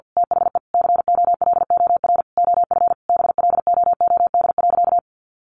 Morse code, heard as a keyed tone, about 50 wpm, 700 Hz. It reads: THE COLOR OF BLOOD.
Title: A Sound Of Thunder 50wpm